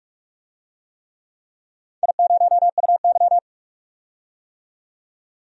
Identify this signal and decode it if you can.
Morse code, heard as a keyed tone, about 44 wpm, 700 Hz. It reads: I0UY
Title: DXCC-100 Call Signs - 4 Characters - 45wpm